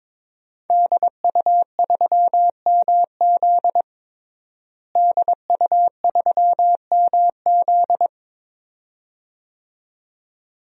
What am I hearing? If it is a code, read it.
Morse code, heard as a keyed tone, about 22 wpm, 700 Hz. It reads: DU3MZ DU3MZ
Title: DXCC-100 Call Signs - 5 Characters - Repeated 2x - 22wpm